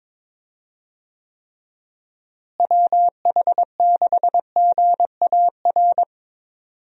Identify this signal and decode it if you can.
Morse code, heard as a keyed tone, about 22 wpm, 700 Hz. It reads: WH6GAR
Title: USA Call Signs - 6 Characters - Repeated 3x - 22wpm